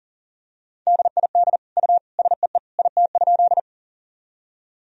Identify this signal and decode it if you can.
Morse code, heard as a keyed tone, about 40 wpm, 700 Hz. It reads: DID U SEE IT?
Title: Sentences from Top 100 Words and Abbreviations 40wpm